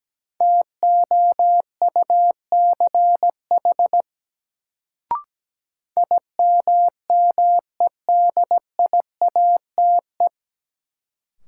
Morse code, heard as a keyed tone, about 17 wpm, 700 Hz. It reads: TOUCH IMMEDIATE
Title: Sentences from Top 1200 Words - New Word Drill - 17wpm